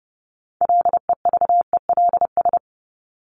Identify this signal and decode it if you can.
Morse code, heard as a keyed tone, about 30 wpm, 700 Hz. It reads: LEVELS
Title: ES:S Suffix - Top 100 Words with Base Words 30wpm